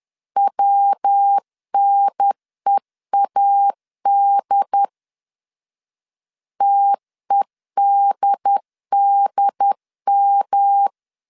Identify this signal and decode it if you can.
Morse code, heard as a keyed tone, about 10 wpm, 800 Hz. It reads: WNEAD TEDDM
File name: CW / DET / 1 3 1500C S C 10wpm 3335